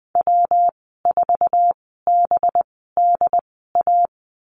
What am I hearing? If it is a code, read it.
Morse code, heard as a keyed tone, about 19 wpm, spaced as for 14 wpm, 700 Hz. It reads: W4BDA
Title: 186 Lesson 30g - Letter b - Call Signs 20wpm